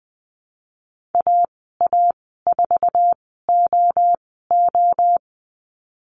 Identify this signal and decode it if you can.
Morse code, heard as a keyed tone, about 20 wpm, spaced as for 14 wpm, 700 Hz. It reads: AA4OO
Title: 055 Lesson 10g - Letter r - Call Signs 20wpm